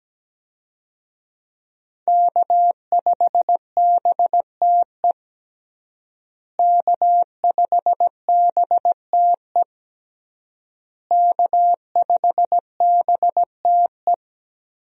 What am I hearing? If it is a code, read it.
Morse code, heard as a keyed tone, about 17 wpm, 700 Hz. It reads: K5BTE K5BTE K5BTE
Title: USA Call Signs - 5 Characters - Repeated 3x - 17wpm